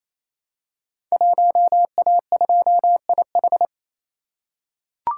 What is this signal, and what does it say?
Morse code, heard as a keyed tone, about 27 wpm, 700 Hz. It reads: 1A2IH
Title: Code Groups - 5 Letter:Numbers - 28wpm